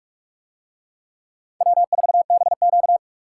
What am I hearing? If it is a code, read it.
Morse code, heard as a keyed tone, about 42 wpm, 700 Hz. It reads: W4BQ